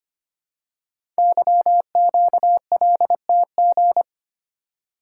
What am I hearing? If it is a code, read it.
Morse code, heard as a keyed tone, about 25 wpm, 700 Hz. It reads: YQLTG
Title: Code Groups - 5 Letters - 25wpm